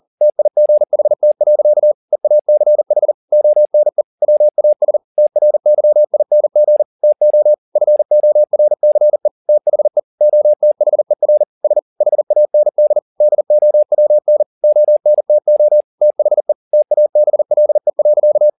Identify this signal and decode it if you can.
Morse code, heard as a keyed tone, about 40 wpm, 600 Hz. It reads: TIGHT. EACH ONE WAS TRYING TO FORCE THE OTHER S HAND DOWN ONTO THE TABLE.